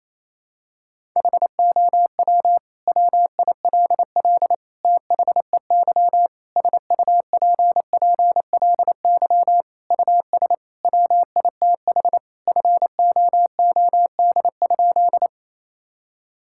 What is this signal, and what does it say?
Morse code, heard as a keyed tone, about 28 wpm, 700 Hz. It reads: HOW WILL THEY SUPPLY US WITH FOOD?
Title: Sentences from Top 1000 Words - Review Part 5 - 28wpm